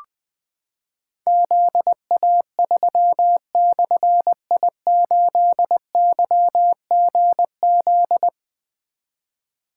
Morse code, heard as a keyed tone, about 20 wpm, 700 Hz. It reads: ZA3/I8YGZ
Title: Difficult Call Signs 20wpm